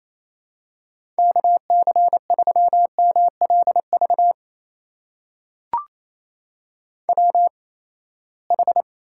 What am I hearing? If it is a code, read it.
Morse code, heard as a keyed tone, about 28 wpm, spaced as for 19 wpm, 700 Hz. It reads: KC3MLV WH
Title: USA Call Signs - 8x Character Spacing - 28wpm